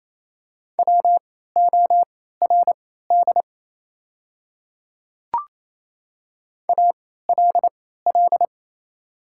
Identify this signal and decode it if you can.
Morse code, heard as a keyed tone, about 28 wpm, spaced as for 16 wpm, 700 Hz. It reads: WORD ALL
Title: 3x Character Spacing - Top 100 Words - 28wpm